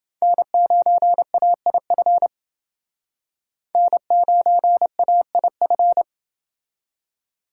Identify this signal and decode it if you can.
Morse code, heard as a keyed tone, about 29 wpm, 700 Hz. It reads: N9AIF N9AIF